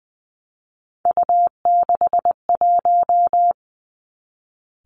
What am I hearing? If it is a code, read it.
Morse code, heard as a keyed tone, about 20 wpm, 700 Hz. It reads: U61
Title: Three Challenging Letter-Number Combinations 20wpm